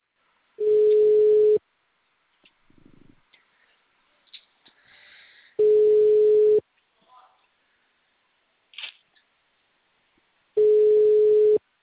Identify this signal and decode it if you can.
Ringback tone (Europe, 425 Hz).